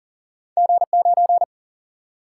Morse code, heard as a keyed tone, about 38 wpm, 700 Hz. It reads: G9
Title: Single-Letter Single-Number 40wpm